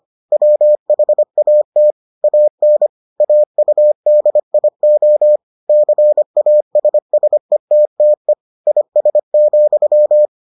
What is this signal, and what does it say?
Morse code, heard as a keyed tone, about 25 wpm, 600 Hz. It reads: WHAT AN AUDIO CASSETTE IS,